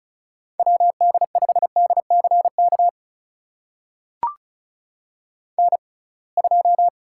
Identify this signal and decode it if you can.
Morse code, heard as a keyed tone, about 36 wpm, spaced as for 27 wpm, 700 Hz. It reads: WD5DCK N2
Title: USA Call Signs - 6x Character Spacing - 35wpm